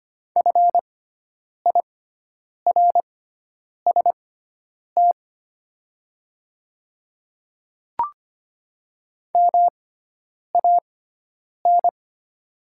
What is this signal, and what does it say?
Morse code, heard as a keyed tone, about 25 wpm, spaced as for 9 wpm, 700 Hz. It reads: FIRST MAN
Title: 6x Character Spacing - Top 300 Words - 25wpm